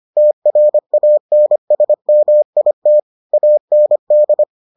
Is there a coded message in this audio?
Yes: Morse code from 0.2 to 4.4 s, heard as a keyed tone, about 25 wpm, 600 Hz. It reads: TRANSMIT AND